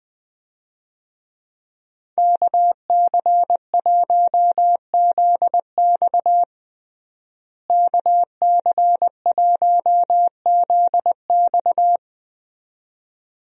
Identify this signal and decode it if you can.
Morse code, heard as a keyed tone, about 20 wpm, 700 Hz. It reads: KC1ZX KC1ZX
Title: USA Call Signs - 5 Characters - Repeated 2x - 20wpm